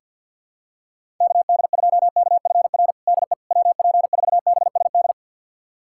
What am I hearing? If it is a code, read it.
Morse code, heard as a keyed tone, about 46 wpm, 700 Hz. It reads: KD2KFR DE WP4BSD
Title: USA Callsign DE Callsign 50wpm